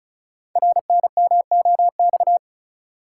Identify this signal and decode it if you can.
Morse code, heard as a keyed tone, about 33 wpm, 700 Hz. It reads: RNMOX
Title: Code Groups - 5 Letters - 35wpm